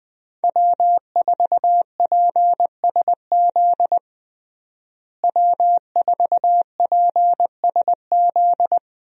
Morse code, heard as a keyed tone, about 20 wpm, 700 Hz. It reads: W4PSZ W4PSZ
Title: USA Call Signs - 5 Characters - Repeated 3x - 20wpm